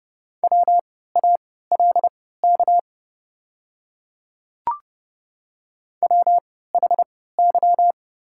Morse code, heard as a keyed tone, about 29 wpm, spaced as for 17 wpm, 690 Hz. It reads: WALK WHY
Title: 3x Character Spacing - Top 300 Words - 30wpm